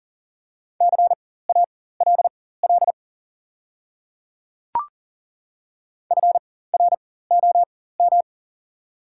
Morse code, heard as a keyed tone, about 40 wpm, spaced as for 19 wpm, 690 Hz. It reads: CALL FROM
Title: 4x Character Spacing - Top 100 Words - 40wpm